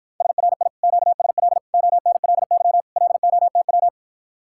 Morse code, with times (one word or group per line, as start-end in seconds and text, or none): 0.2-0.7 SRI
0.8-1.6 QSL
1.7-2.8 ONLY
3.0-3.9 LOTW